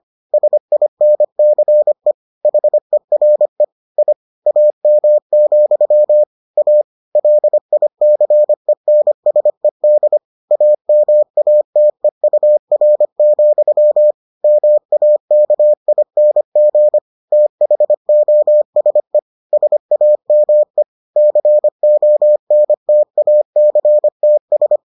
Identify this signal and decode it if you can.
Morse code, heard as a keyed tone, about 25 wpm, 600 Hz. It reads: SINCE HERE I AM, A LICENSED AMATEUR, MAKING THOSE SAME CONTACTS